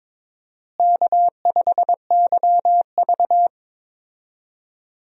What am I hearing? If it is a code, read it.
Morse code, heard as a keyed tone, about 22 wpm, 700 Hz. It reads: K5YV
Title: USA Call Signs - 4 Characters - Repeated 2x - 22wpm